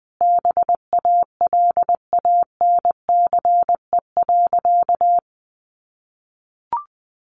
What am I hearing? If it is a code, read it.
Morse code, heard as a keyed tone, about 20 wpm, 700 Hz. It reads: BALANCE.
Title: A Sound Of Thunder 20wpm